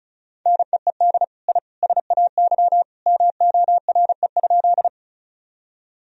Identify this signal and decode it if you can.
Morse code, heard as a keyed tone, about 36 wpm, 700 Hz. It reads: NEED I SAY MORE?
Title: Sentences from Top 1000 Words - Review Part 1 - 35wpm